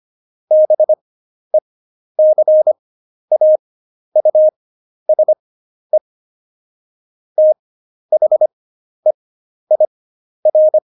Morse code, heard as a keyed tone, about 25 wpm, spaced as for 11 wpm, 620 Hz. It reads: BECAUSE THEIR
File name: Jokes-25-char-10-eff